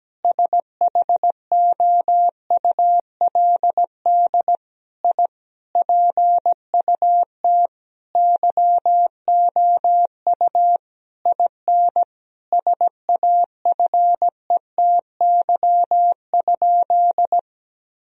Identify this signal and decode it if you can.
Morse code, heard as a keyed tone, about 17 wpm, 700 Hz. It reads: SHOULD I PUT YOU IN SAFETY?
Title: Sentences from Top 1000 Words - Review Part 4 - 17wpm